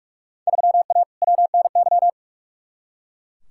Morse code, heard as a keyed tone, about 43 wpm, 700 Hz. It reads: 3A WNY
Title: Field Day Exchanges - 45wpm